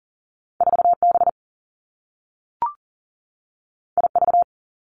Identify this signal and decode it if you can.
Morse code, heard as a keyed tone, about 41 wpm, 700 Hz. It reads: 4 B I V
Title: Two Challenging Letter-Number Combinations 40wpm